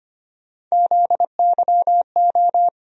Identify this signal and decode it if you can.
Morse code, heard as a keyed tone, about 25 wpm, 700 Hz. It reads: ZYO
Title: USA Call Sign Postfixes - 3 Letters 25wpm